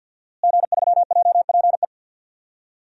Morse code, heard as a keyed tone, about 48 wpm, 700 Hz. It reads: G3JPE
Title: DXCC-100 Call Signs - 5 Characters - Repeated 3x - 50wpm